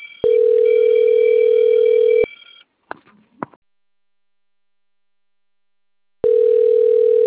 Ringback tone (North America), 440 + 480 Hz.